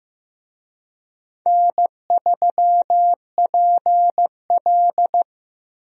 Morse code, heard as a keyed tone, about 15 wpm, 700 Hz. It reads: N3PL
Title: USA Call Signs - 4 Characters - Repeated 3x - 15wpm